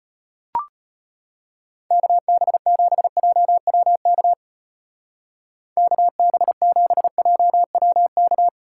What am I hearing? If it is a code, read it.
Morse code, heard as a keyed tone, about 33 wpm, 700 Hz. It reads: KB7JWK KB7JWK